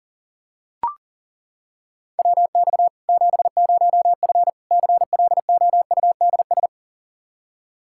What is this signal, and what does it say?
Morse code, heard as a keyed tone, about 37 wpm, 700 Hz. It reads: WX 70F CLOUDS
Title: Practice QSO - Exchange 2 - Info - 40wpm